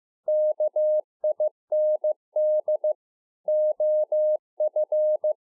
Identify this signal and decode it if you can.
Morse code, heard as a keyed tone, about 15 wpm, 610 Hz. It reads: KIND OF